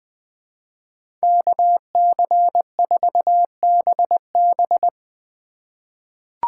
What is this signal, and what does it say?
Morse code, heard as a keyed tone, about 20 wpm, 700 Hz. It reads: KC4BB
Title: 2023 POTA Activator Call Signs - 20wpm